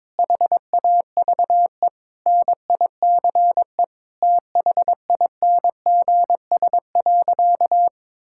Morse code, heard as a keyed tone, about 22 wpm, 700 Hz. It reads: HAVE NICE THINGS.
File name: practice / 22wpm / Sentences with 8-Word Phrases - 22wpm